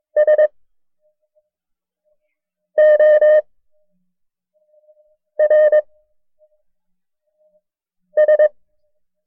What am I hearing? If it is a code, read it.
Morse code, heard as a keyed tone, about 22 wpm, 610 Hz. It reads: S O R S
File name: FreeFiles / CW-practice / KB6MT-tape1-side2